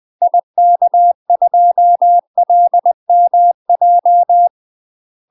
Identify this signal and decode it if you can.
Morse code, heard as a keyed tone, about 20 wpm, 700 Hz. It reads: IK2LMJ
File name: UK MASTER QSOs TRAINING / 20WPM / UK MASTER QSOs 0008